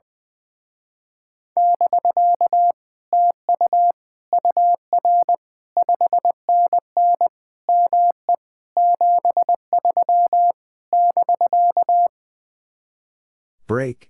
Morse code, heard as a keyed tone, about 20 wpm, 700 Hz. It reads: <BK> TU UR 5NN ME 73 <BK>
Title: Speed-Racing Common QSO Phrases Encore - 20wpm